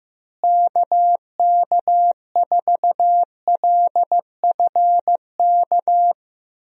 Morse code, heard as a keyed tone, about 15 wpm, 700 Hz. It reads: KK4LFK
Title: USA Call Signs - 6 Characters - Repeated 3x - 15wpm